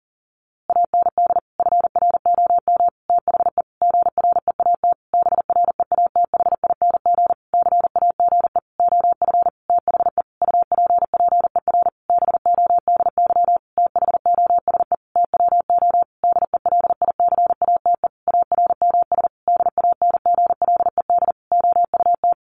Morse code, heard as a keyed tone, about 40 wpm, 700 Hz. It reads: AND FROM THE GREAT BREATHING CAGE OF THE UPPER BODY THOSE TWO DELICATE ARMS DANGLED OUT
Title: A Sound Of Thunder 40wpm